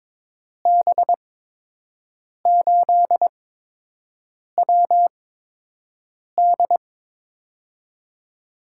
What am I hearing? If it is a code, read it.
Morse code, heard as a keyed tone, about 22 wpm, spaced as for 6 wpm, 700 Hz. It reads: B8WD
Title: USA Call Signs - 5 Characters - 8x Character Spacing - 22wpm